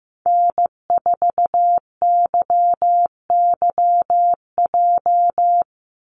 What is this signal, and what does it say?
Morse code, heard as a keyed tone, about 15 wpm, 700 Hz. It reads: N4YYJ
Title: USA Call Signs Repeated 3x 15wpm